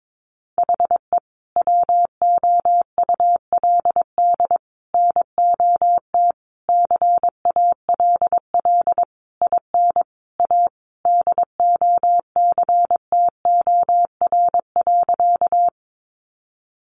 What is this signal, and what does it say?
Morse code, heard as a keyed tone, about 22 wpm, 700 Hz. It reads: HE WOULD NOT CALL IN A DOCTOR.
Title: Sentences from Top 1500 Words 22wpm